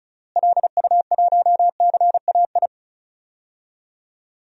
Morse code, heard as a keyed tone, about 36 wpm, 700 Hz. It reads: LU1CAI
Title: DXCC-100 Call Signs - 6 Characters - Repeated 3x - 35wpm